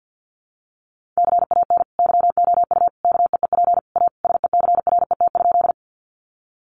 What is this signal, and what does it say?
Morse code, heard as a keyed tone, about 50 wpm, 700 Hz. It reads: CAN YOU KEEP A SECRET?
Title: Sentences from Top 1200 Words 50wpm